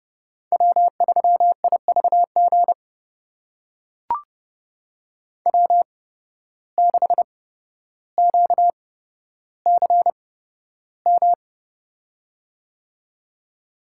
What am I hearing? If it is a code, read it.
Morse code, heard as a keyed tone, about 30 wpm, spaced as for 13 wpm, 700 Hz. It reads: W3IVG W6QCM
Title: USA Call Signs - 8x Character Spacing - 30wpm